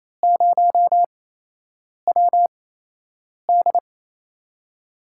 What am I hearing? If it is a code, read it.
Morse code, heard as a keyed tone, about 28 wpm, spaced as for 8 wpm, 700 Hz. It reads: 0WD
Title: USA Call Signs - 4 Characters - 8x Character Spacing - 28wpm